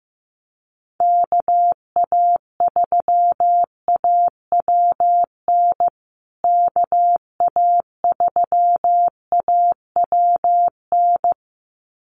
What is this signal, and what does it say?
Morse code, heard as a keyed tone, about 15 wpm, 700 Hz. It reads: KA3AWN KA3AWN
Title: USA Call Signs Repeated 2x 15wpm